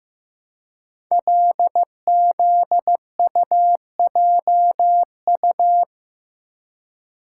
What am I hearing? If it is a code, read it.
Morse code, heard as a keyed tone, about 15 wpm, 700 Hz. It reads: LZUJU
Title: Code Groups - 5 Letters - 15wpm